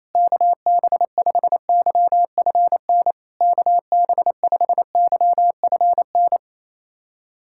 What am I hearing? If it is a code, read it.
Morse code, heard as a keyed tone, about 28 wpm, 700 Hz. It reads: KB5YFN KB5YFN
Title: SOTA QSO Practice 28wpm